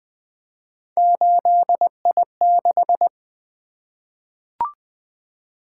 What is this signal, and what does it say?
Morse code, heard as a keyed tone, about 20 wpm, 700 Hz. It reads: 8I6
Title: Three Letter-Number Combinations - 20wpm